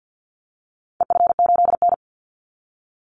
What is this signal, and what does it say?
Morse code, heard as a keyed tone, about 50 wpm, 700 Hz. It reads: EF8N
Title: Call Signs - 50wpm